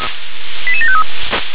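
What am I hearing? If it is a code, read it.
Five-tone selective call (ZVEI), heard as quick runs of five tones, tones of about 70 ms. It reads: 88753
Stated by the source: ZVEI five-tone selective call 88727 received by SDR.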